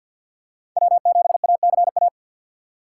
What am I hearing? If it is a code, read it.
Morse code, heard as a keyed tone, about 52 wpm, 700 Hz. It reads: W7ACA
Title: USA Call Signs - 5 Characters - 3x Character Spacing - 50wpm